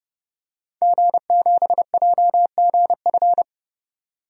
Morse code, heard as a keyed tone, about 30 wpm, 700 Hz. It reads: G7JGF
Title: DXCC-100 Call Signs - 5 Characters - 30wpm